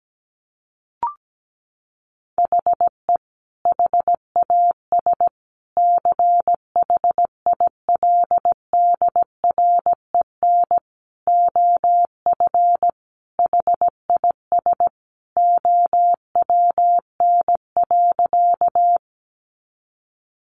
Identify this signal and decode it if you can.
Morse code, heard as a keyed tone, about 17 wpm, 700 Hz. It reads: HE HAS CHILDREN OF HIS OWN.
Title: Sentences from Top 300 Words 17wpm